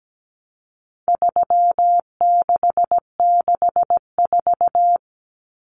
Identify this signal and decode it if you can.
Morse code, heard as a keyed tone, about 17 wpm, 700 Hz. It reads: 3664
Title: Common Cut Numbers - 4-Digit 17wpm